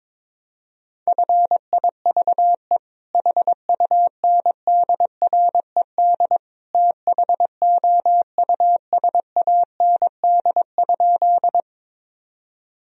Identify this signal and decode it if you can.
Morse code, heard as a keyed tone, about 22 wpm, 700 Hz. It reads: FIVE HUNDRED THOUSAND?
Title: Sentences from Top 1000 Words - Review Part 3 - 22wpm